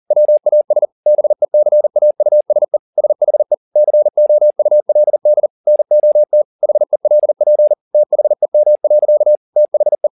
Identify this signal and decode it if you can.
Morse code, heard as a keyed tone, about 40 wpm, 600 Hz. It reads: WAS BECAUSE SHE COULD NOT HELP THEM. THE